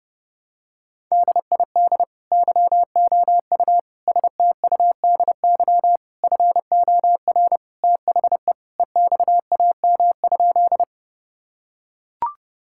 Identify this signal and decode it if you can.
Morse code, heard as a keyed tone, about 30 wpm, 700 Hz. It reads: DID YOU STUDY FOR THE EXAM?
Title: Sentences from Top 1000 Words - Review Part 5 - 30wpm